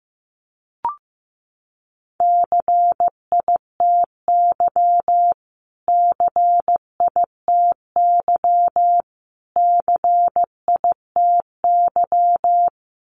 Morse code, heard as a keyed tone, about 15 wpm, 700 Hz. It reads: CITY CITY CITY
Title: Top 500 Words Repeated 15wpm